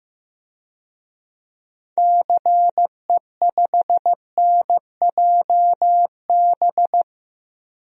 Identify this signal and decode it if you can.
Morse code, heard as a keyed tone, about 15 wpm, 700 Hz. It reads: CE5NJB
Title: DXCC-100 Call Signs - 6 Characters - Repeated 3x - 15wpm